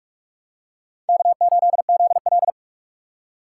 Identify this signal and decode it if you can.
Morse code, heard as a keyed tone, about 46 wpm, 700 Hz. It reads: K8ZL
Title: USA Call Signs - 4 Characters - 2x Character Spacing - 45wpm